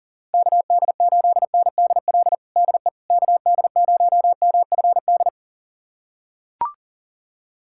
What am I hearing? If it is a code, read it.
Morse code, heard as a keyed tone, about 37 wpm, 700 Hz. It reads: KD8NDL DE KD0MFD
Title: USA Callsign DE Callsign 40wpm